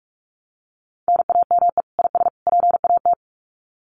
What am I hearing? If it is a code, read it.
Morse code, heard as a keyed tone, about 44 wpm, 700 Hz. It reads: NAME IS PAT
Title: Common QSO Phrases 45wpm